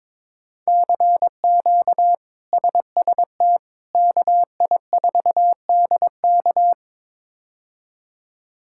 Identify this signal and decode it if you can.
Morse code, heard as a keyed tone, about 22 wpm, 700 Hz. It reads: CQ SST KI4DK
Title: K1USN SST QSO Practice - 22wpm